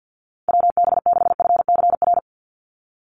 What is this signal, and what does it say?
Morse code, heard as a keyed tone, about 50 wpm, 700 Hz. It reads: WB6FCR
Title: USA Call Signs 50wpm